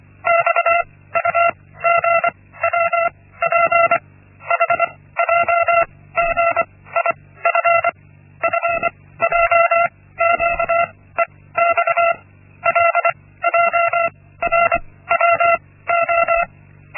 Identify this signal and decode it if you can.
Morse code, heard as a keyed tone, about 24 wpm, spaced as for 16 wpm, 680 Hz. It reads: XUGWP HJGIF FJQEX LJRWO
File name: Linksmakalnis-1